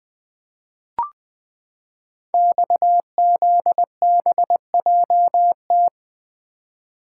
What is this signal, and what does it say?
Morse code, heard as a keyed tone, about 20 wpm, 700 Hz. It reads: XZBJT
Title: Code Groups - 5 Letter:Numbers - 20wpm